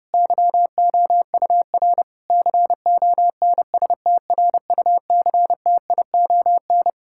Morse code, heard as a keyed tone, about 30 wpm, 700 Hz. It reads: YOUR CONSTRUCTION
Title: Sentences from Top 1600 Words 30wpm